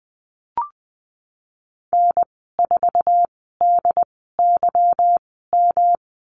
Morse code, heard as a keyed tone, about 20 wpm, spaced as for 15 wpm, 700 Hz. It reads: N4DYM
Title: 147 Lesson 24g - Letter y - Call Signs 20wpm